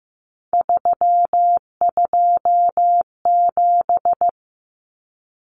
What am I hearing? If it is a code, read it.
Morse code, heard as a keyed tone, about 15 wpm, 700 Hz. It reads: 327
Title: Common Cut Numbers - 3-Digit 15wpm